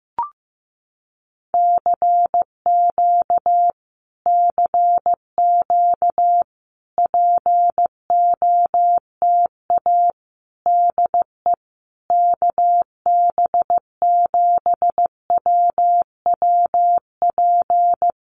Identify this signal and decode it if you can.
Morse code, heard as a keyed tone, about 15 wpm, 700 Hz. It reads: CQ CQ POTA DE KB7WWP
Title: Common QSO Phrases Encore 15wpm